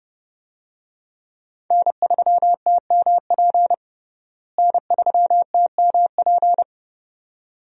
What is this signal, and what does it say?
Morse code, heard as a keyed tone, about 32 wpm, 700 Hz. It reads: N3TMP N3TMP
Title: USA Call Signs - 5 Characters - Repeated 2x - 30wpm